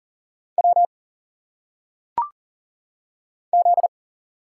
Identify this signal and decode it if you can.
Morse code, heard as a keyed tone, about 40 wpm, 700 Hz. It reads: W Z
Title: Single Letters - Phonetic Pronunciation 40wpm